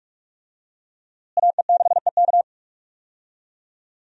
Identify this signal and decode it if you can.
Morse code, heard as a keyed tone, about 46 wpm, 700 Hz. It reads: AE6EK
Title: USA Call Signs - 5 Characters - Repeated 3x - 45wpm